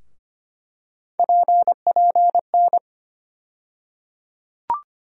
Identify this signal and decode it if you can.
Morse code, heard as a keyed tone, about 25 wpm, 700 Hz. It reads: PPN
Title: USA Call Sign Postfixes - 3 Letters 25wpm